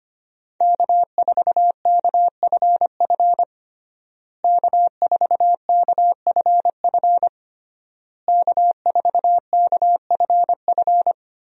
Morse code, heard as a keyed tone, about 25 wpm, 700 Hz. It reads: K4KFF K4KFF K4KFF
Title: USA Call Signs - 5 Characters - Repeated 3x - 25wpm